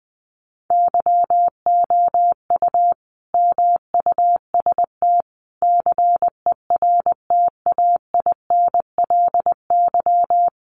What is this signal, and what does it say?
Morse code, heard as a keyed tone, about 19 wpm, 700 Hz. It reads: YOU MUST CERTAINLY
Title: Sentences from Top 900 Words 20wpm